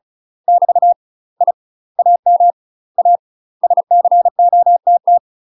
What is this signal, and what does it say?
Morse code, heard as a keyed tone, about 36 wpm, 700 Hz. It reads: <BT> I AM A SCOTT
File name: UK MASTER QSOs TRAINING / 35WPM / UK MASTER QSOs 0038